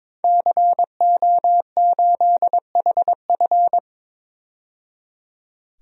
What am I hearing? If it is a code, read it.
Morse code, heard as a keyed tone, about 22 wpm, 700 Hz. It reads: CO8HF
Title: DXCC-100 Call Signs - 5 Characters - 22wpm